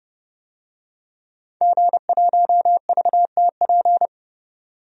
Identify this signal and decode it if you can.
Morse code, heard as a keyed tone, about 30 wpm, 690 Hz. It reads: G1VTP
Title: DXCC-100 Call Signs - 5 Characters - 30wpm